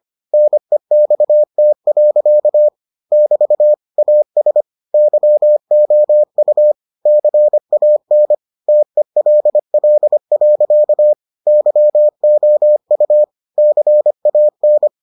Morse code, heard as a keyed tone, about 25 wpm, 600 Hz. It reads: NEXT. <BT> AS YOU CAN TELL. YOU CAN